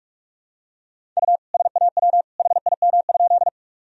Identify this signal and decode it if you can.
Morse code, heard as a keyed tone, about 46 wpm, 700 Hz. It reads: U SAW HIM?
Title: Sentences from Top 300 Words and Abbreviations 45wpm